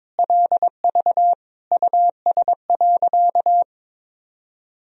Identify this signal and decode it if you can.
Morse code, heard as a keyed tone, about 22 wpm, 700 Hz. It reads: LV US.
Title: Sentences from Top 400 Words and Abbreviations 22wpm